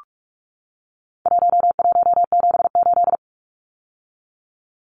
Morse code, heard as a keyed tone, about 44 wpm, 700 Hz. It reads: 1178
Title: Numbers - 4-Digit 45wpm